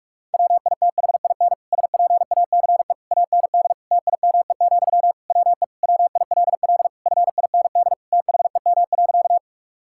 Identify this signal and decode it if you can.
Morse code, heard as a keyed tone, about 46 wpm, 700 Hz. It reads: WITHIN SPACE AND TIME, WE WILL FIND THEM.
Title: Sentences from Top 1000 Words - Review Part 5 - 45wpm